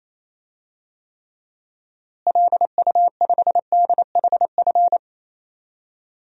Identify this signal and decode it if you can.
Morse code, heard as a keyed tone, about 28 wpm, 700 Hz. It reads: LU5DHF
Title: DXCC-100 Call Signs - 6 Characters - Repeated 3x - 28wpm